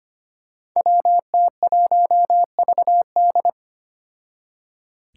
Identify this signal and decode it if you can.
Morse code, heard as a keyed tone, about 25 wpm, 700 Hz. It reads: WT1VD
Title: Five Challenging Letter-Number Combinations 25wpm